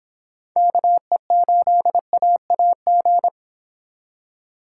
Morse code, heard as a keyed tone, about 26 wpm, 700 Hz. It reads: KE8AAG